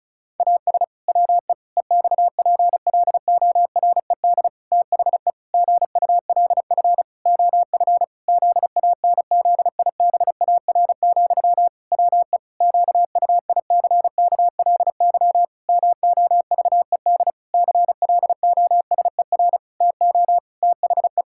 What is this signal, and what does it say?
Morse code, heard as a keyed tone, about 34 wpm, 700 Hz. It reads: AS WE EXPLORED THE GULF OF ZANZIBAR, WE QUICKLY MOVED CLOSER TO THE
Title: Pangram Challenge v2 - 35wpm